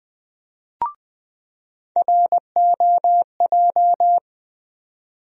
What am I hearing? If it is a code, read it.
Morse code, heard as a keyed tone, about 20 wpm, 700 Hz. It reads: ROJ
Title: USA Call Sign Postfixes - 3 Letters 20wpm